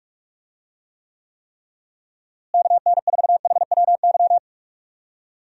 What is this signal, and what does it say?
Morse code, heard as a keyed tone, about 42 wpm, 700 Hz. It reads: KN4HWY
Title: USA Call Signs - 6 Characters - Repeated 3x - 45wpm